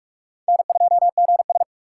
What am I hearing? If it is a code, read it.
Morse code, heard as a keyed tone, about 41 wpm, 700 Hz. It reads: N2GS
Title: USA Call Signs - 4 Characters - 4x Character Spacing - 45wpm